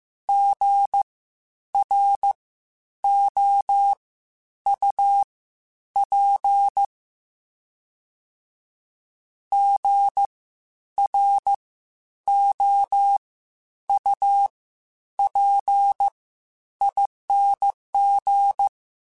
Morse code, heard as a keyed tone, about 15 wpm, spaced as for 9 wpm, 800 Hz. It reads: GROUP GROUPING
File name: Spaced Alphas / Spaced Alphas mp3s / Gs sp @13